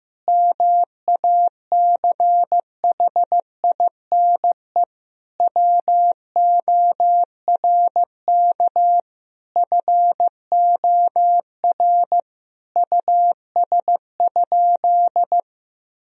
Morse code, heard as a keyed tone, about 16 wpm, 700 Hz. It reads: MACHINE WORK FOR US?
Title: Sentences from Top 1000 Words - Review Part 2 - 15wpm